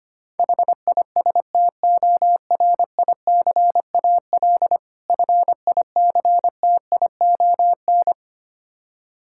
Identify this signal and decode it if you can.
Morse code, heard as a keyed tone, about 25 wpm, 700 Hz. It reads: HISTORICAL FICTION
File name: practice / 25wpm / Difficult Words with Hints - 25wpm